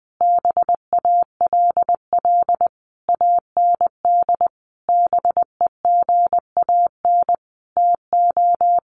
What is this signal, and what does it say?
Morse code, heard as a keyed tone, about 20 wpm, 700 Hz. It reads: BALL AND BEGAN TO
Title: Sentences from Top 900 Words 20wpm